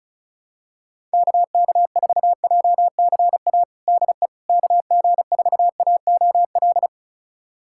Morse code, heard as a keyed tone, about 34 wpm, 700 Hz. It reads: KK4JCA DE KG4AOL
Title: USA Callsign DE Callsign - 6 Characters - 35wpm